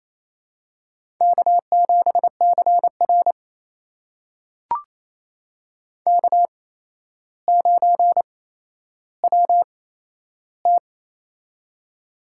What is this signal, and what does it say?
Morse code, heard as a keyed tone, about 27 wpm, spaced as for 12 wpm, 700 Hz. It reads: K7CR K9WT
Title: USA Call Signs - 4 Characters - 8x Character Spacing - 28wpm